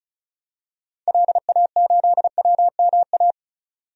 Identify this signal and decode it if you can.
Morse code, heard as a keyed tone, about 34 wpm, 700 Hz. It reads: LA8WMA
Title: DXCC-100 Call Signs - 6 Characters - 35wpm